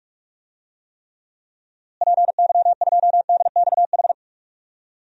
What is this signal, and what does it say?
Morse code, heard as a keyed tone, about 42 wpm, 700 Hz. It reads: PY2DXH
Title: DXCC-100 Call Signs - 6 Characters - Repeated 3x - 45wpm